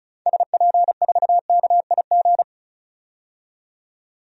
Morse code, heard as a keyed tone, about 36 wpm, 700 Hz. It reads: SP4KIG
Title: DXCC-100 Call Signs - 6 Characters - 35wpm